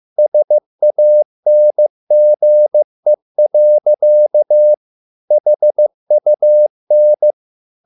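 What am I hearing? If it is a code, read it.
Morse code, heard as a keyed tone, about 15 wpm, 600 Hz. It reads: SANGE. HUN
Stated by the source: Morse code at a slow 15 wpm from CW med Gnister.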